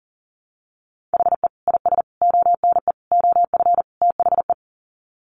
Morse code, heard as a keyed tone, about 40 wpm, 700 Hz. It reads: HE IS ONE OF THE
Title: Sets of 5 Words Encore 40wpm